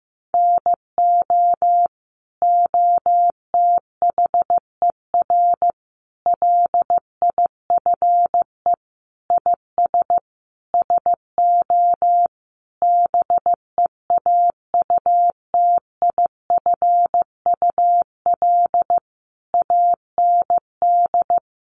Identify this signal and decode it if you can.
Morse code, heard as a keyed tone, about 15 wpm, 700 Hz. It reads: NO OTHER LIFE IS SO BEAUTIFUL AND
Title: Sentences from Top 1500 Words 15wpm